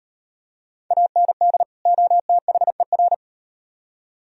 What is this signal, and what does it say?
Morse code, heard as a keyed tone, about 38 wpm, 700 Hz. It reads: AND OTHER